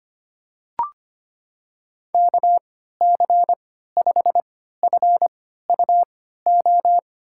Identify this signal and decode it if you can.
Morse code, heard as a keyed tone, about 25 wpm, spaced as for 14 wpm, 700 Hz. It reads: KC5FUO
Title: USA Call Signs - 6 Characters - 3x Character Spacing - 25wpm